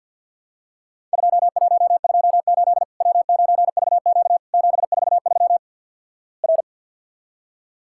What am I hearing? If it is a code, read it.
Morse code, heard as a keyed tone, about 53 wpm, 700 Hz. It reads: 2128 W9VQ 743 R
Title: Worked All Europe DX Contest - QTC Exchange - 50wpm